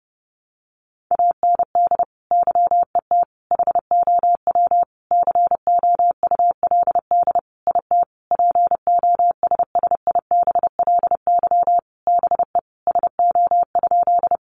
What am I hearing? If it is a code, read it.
Morse code, heard as a keyed tone, about 30 wpm, 700 Hz. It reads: AND YET HOW COULD IT POSSIBLY BE SO?
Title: Sentences from Top 1300 Words 30wpm